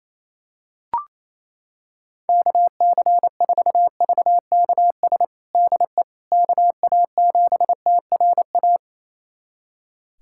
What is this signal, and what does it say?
Morse code, heard as a keyed tone, about 28 wpm, 700 Hz. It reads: KC4VKS DE KA7TRA
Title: USA Callsign DE Callsign 28wpm